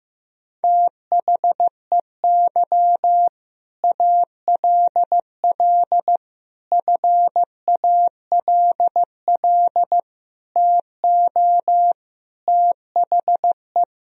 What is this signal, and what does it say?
Morse code, heard as a keyed tone, about 15 wpm, 700 Hz. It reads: THEY ALL FALL TO THE
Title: Sentences from Top 500 Words and Abbreviations 15wpm